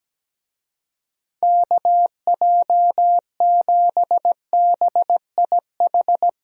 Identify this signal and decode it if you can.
Morse code, heard as a keyed tone, about 17 wpm, 700 Hz. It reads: KJ7BIH
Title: USA Call Signs - 6 Characters - Repeated 3x - 17wpm